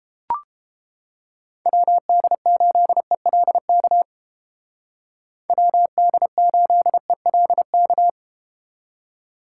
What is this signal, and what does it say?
Morse code, heard as a keyed tone, about 30 wpm, 700 Hz. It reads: WD8ELK WD8ELK